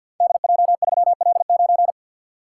Morse code, heard as a keyed tone, about 53 wpm, 700 Hz. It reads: DJ3L9
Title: Code Groups - 5 Letter:Numbers - 50wpm